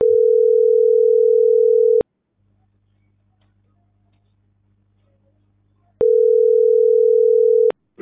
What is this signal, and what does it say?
Ringback tone (North America), 440 + 480 Hz.